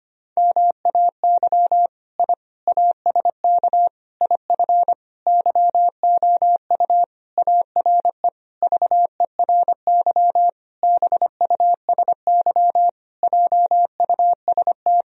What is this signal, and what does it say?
Morse code, heard as a keyed tone, about 25 wpm, 700 Hz. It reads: MAY I ASK IF YOU ARE VERY BUSY JUST
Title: Sentences from Top 1000 Words - Review Part 3 - 25wpm